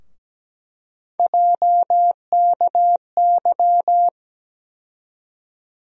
Morse code, heard as a keyed tone, about 17 wpm, 700 Hz. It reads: JKY
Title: USA Call Sign Postfixes - 3 Letters 17wpm